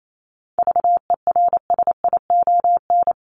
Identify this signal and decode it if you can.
Morse code, heard as a keyed tone, about 28 wpm, 700 Hz. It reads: VERSION
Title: Top 1251-1500 Words 28wpm